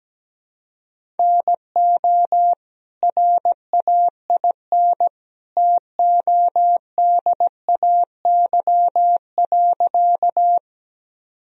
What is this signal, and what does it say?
Morse code, heard as a keyed tone, about 17 wpm, 700 Hz. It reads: NO RAIN TODAY.
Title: Sentences from Top 1000 Words - Review Part 1 - 17wpm